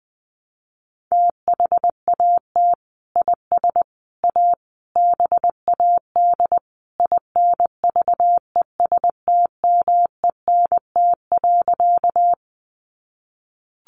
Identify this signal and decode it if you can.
Morse code, heard as a keyed tone, about 20 wpm, 700 Hz. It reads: THAT IS A BAD INVESTMENT.
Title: Sentences from Top 700 Words 20wpm